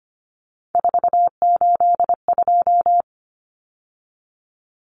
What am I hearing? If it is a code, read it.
Morse code, heard as a keyed tone, about 25 wpm, 700 Hz. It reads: 482
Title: Common Cut Numbers - 3-Digit 25wpm